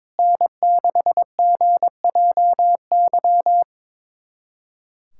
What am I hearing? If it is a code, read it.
Morse code, heard as a keyed tone, about 22 wpm, 700 Hz. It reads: N6GJY
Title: USA Callsign DE Callsign - 5 Characters - 22wpm